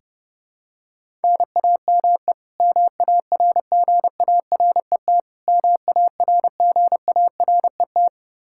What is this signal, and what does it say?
Morse code, heard as a keyed tone, about 30 wpm, 700 Hz. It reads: NAME MARGARET MARGARET
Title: K1USN SST QSO Practice 30wpm